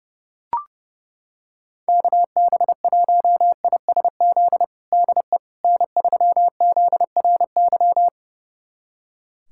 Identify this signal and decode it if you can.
Morse code, heard as a keyed tone, about 30 wpm, 700 Hz. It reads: KB1ISZ DE N3ZRY
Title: USA Callsign DE Callsign 30wpm